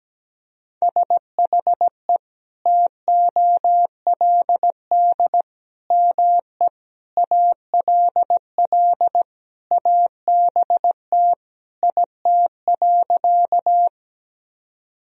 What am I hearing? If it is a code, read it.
Morse code, heard as a keyed tone, about 17 wpm, 700 Hz. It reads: SHE TOLD ME ALL ABT IT.
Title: Sentences from Top 400 Words and Abbreviations 17wpm